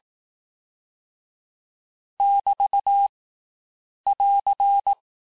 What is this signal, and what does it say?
Morse code, heard as a keyed tone, about 18 wpm, 800 Hz. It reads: <BT> <AR>
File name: cours-du-lundi files / cours-1852-abreviations-vitesse-17MPM-puis-22MPM